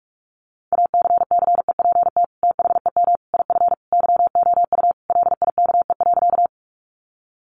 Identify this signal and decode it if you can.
Morse code, heard as a keyed tone, about 44 wpm, 700 Hz. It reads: ACCEPT THEM IF YOU LIKE.
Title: Sentences from Top 1300 Words 45wpm